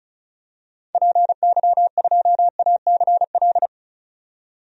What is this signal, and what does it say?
Morse code, heard as a keyed tone, about 36 wpm, 700 Hz. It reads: PY2ACL
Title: DXCC-100 Call Signs - 6 Characters - 35wpm